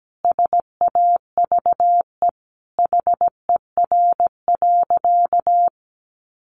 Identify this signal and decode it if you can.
Morse code, heard as a keyed tone, about 17 wpm, 700 Hz. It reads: SAVE HER.
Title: Sentences from Top 1100 Words 17wpm